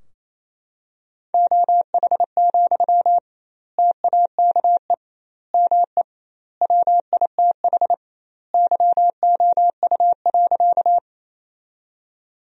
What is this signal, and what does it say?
Morse code, heard as a keyed tone, about 28 wpm, 700 Hz. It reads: OH, TAKE ME WITH YOU.
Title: 2x Word-Spacing - Sentences from Top 500 Words - 28wpm